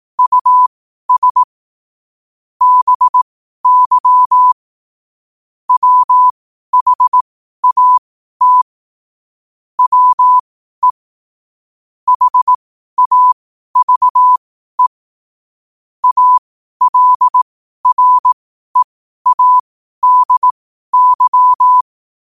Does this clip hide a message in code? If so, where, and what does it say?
Yes: Morse code from 0.2 to 21.8 s, heard as a keyed tone, about 18 wpm, spaced as for 12 wpm, 1000 Hz. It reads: US BY WHAT WE HAVE ALREADY